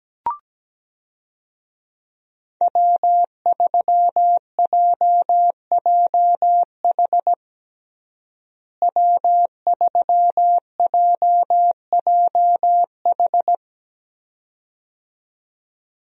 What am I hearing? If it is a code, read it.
Morse code, heard as a keyed tone, about 17 wpm, 700 Hz. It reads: W3JJH W3JJH
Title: USA Call Signs - 5 Characters - Repeated 2x - 17wpm